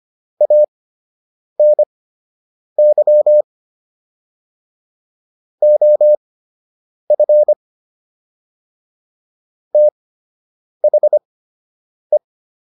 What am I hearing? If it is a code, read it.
Morse code, heard as a keyed tone, about 25 wpm, spaced as for 8 wpm, 600 Hz. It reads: ANY OF THE